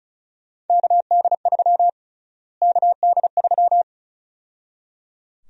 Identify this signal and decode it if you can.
Morse code, heard as a keyed tone, about 34 wpm, 700 Hz. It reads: KD3 KD3
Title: USA Call Sign Prefixes - Repeated 2x - 35wpm